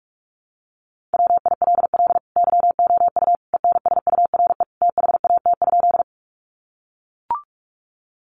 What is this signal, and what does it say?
Morse code, heard as a keyed tone, about 44 wpm, 700 Hz. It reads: WILL YOU ENSURE THAT?
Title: Sentences from Top 1100 Words 45wpm